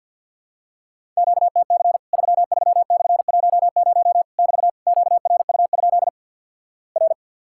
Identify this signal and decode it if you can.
Morse code, heard as a keyed tone, about 49 wpm, 700 Hz. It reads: QTC 33/10 <BT> QRU? R
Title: Worked All Europe DX Contest - QTC Exchange - 50wpm